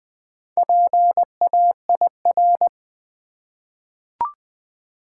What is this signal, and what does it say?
Morse code, heard as a keyed tone, about 20 wpm, 700 Hz. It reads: PAIR
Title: Four Letter Words - 20wpm